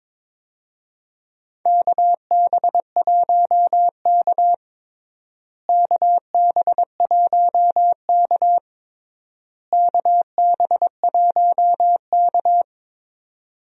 Morse code, heard as a keyed tone, about 22 wpm, 700 Hz. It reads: KB1K KB1K KB1K
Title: USA Call Signs - 4 Characters - Repeated 3x - 22wpm